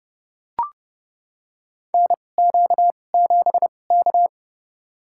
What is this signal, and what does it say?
Morse code, heard as a keyed tone, about 30 wpm, spaced as for 22 wpm, 700 Hz. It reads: NQ7K
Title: USA Call Signs - 4 Characters - 2x Character Spacing - 30wpm